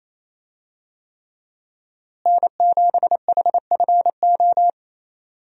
Morse code, heard as a keyed tone, about 28 wpm, 700 Hz. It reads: N7HFO
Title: USA Call Signs - 5 Characters - Repeated 3x - 28wpm